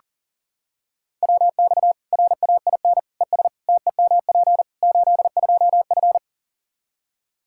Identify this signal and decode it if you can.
Morse code, heard as a keyed tone, about 40 wpm, 700 Hz. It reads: WX RAIN ES TEMP 82F
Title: Practice QSO - Exchange 2 - Info - 40wpm